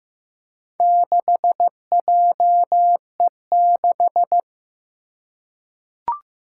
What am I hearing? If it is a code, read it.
Morse code, heard as a keyed tone, about 15 wpm, 700 Hz. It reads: 6JE6
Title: Four Challenging Letter-Number Combinations 15wpm